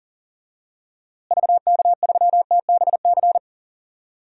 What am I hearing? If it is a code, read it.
Morse code, heard as a keyed tone, about 37 wpm, 700 Hz. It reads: VK3TBC
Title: DXCC-100 Call Signs - 6 Characters - Repeated 3x - 40wpm